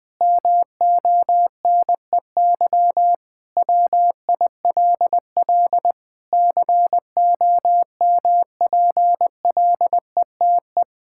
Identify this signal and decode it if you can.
Morse code, heard as a keyed tone, about 20 wpm, 700 Hz. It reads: MONEY WILL COMPLETE
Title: Sentences from Top 1000 Words - Review Part 5 - 20wpm